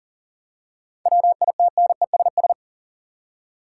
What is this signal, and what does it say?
Morse code, heard as a keyed tone, about 40 wpm, 700 Hz. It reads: WITNESS
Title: Seven Letter Words - 40wpm